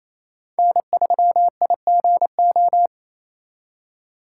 Morse code, heard as a keyed tone, about 34 wpm, 700 Hz. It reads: N3IGO
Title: Speed Racing - USA 1x3 Call Signs 28wpm